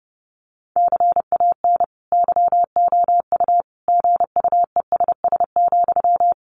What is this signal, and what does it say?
Morse code, heard as a keyed tone, about 30 wpm, 700 Hz. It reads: CAN YOU GUESS,
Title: Sentences from Top 1500 Words 30wpm